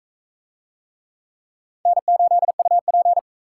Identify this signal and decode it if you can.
Morse code, heard as a keyed tone, about 43 wpm, 700 Hz. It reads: N8UP